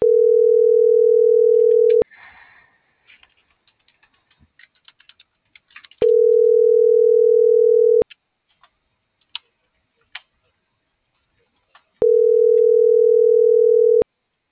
Ringback tone (North America), 440 + 480 Hz.